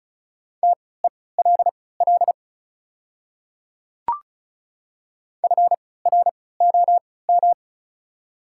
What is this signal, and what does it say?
Morse code, heard as a keyed tone, about 36 wpm, spaced as for 20 wpm, 690 Hz. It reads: TELL FROM
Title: 3x Character Spacing - Top 200 Words - 35wpm